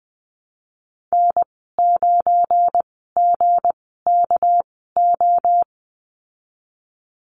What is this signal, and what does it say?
Morse code, heard as a keyed tone, about 20 wpm, spaced as for 15 wpm, 700 Hz. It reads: N9GKO
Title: 245 Lesson 39g - Letter z - Call Signs 20wpm